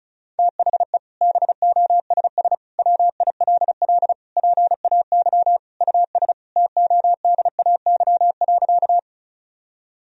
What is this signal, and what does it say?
Morse code, heard as a keyed tone, about 36 wpm, 700 Hz. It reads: THE BOSS WILL PAY US TODAY.
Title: Sentences from Top 1000 Words - Review Part 4 - 35wpm